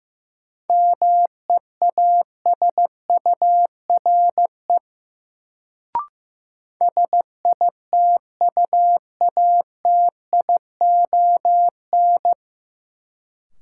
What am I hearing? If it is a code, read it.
Morse code, heard as a keyed tone, about 15 wpm, 700 Hz. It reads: MEASURE SITUATION
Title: Sentences from Top 1200 Words - New Word Drill Alt - 15wpm